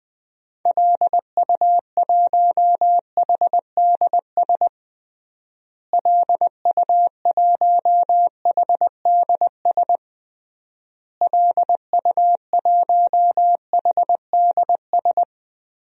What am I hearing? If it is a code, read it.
Morse code, heard as a keyed tone, about 20 wpm, 700 Hz. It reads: LU1HDS LU1HDS LU1HDS
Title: DXCC-100 Call Signs - 6 Characters - Repeated 3x - 20wpm